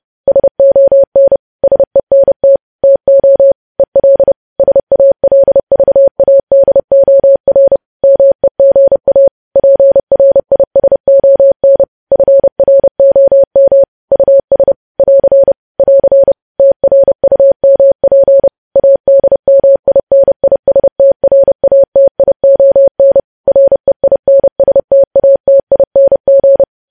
Morse code, heard as a keyed tone, about 30 wpm, 550 Hz. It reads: SON SENT TO EL SALVADOR MEGA PRISON FROM US <AR> <AR> TRUMP ADMINISTRATION REINSTATING
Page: news headlines in morse code at 30 wpm